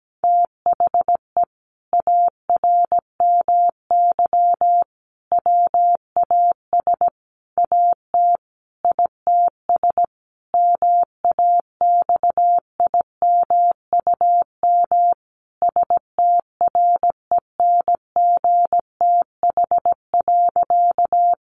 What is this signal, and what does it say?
Morse code, heard as a keyed tone, about 17 wpm, 700 Hz. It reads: THE ARMY WAS AT ITS MAXIMUM STRENGTH.
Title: Sentences from Top 1300 Words 17wpm